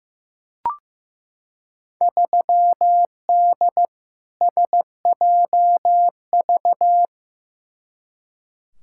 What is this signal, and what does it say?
Morse code, heard as a keyed tone, about 15 wpm, 700 Hz. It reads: 3D SJV
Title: Field Day Exchanges - 15wpm